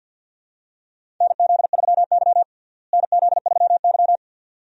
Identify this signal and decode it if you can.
Morse code, heard as a keyed tone, about 50 wpm, 700 Hz. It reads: NZ3Y NZ3Y
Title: USA Call Signs - 4 Characters - Repeated 2x - 50wpm